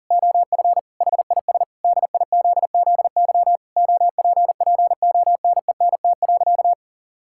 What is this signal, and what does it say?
Morse code, heard as a keyed tone, about 37 wpm, 700 Hz. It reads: OF HIS DIZZY OPPONENT.
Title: Pangram Challenge v2 - 40wpm